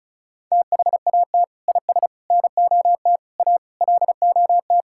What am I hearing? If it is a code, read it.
Morse code, heard as a keyed tone, about 36 wpm, 700 Hz. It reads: THAT IS NOT A LOT
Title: Sentences from Top 1000 Words - Review Part 5 - 35wpm